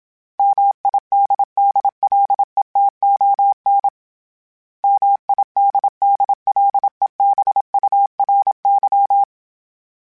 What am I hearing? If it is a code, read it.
Morse code, heard as a keyed tone, about 27 wpm, 800 Hz. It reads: MIDDLETON MIDDLEBURY
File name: fromSMs / FromSMs mp3s / Cities@25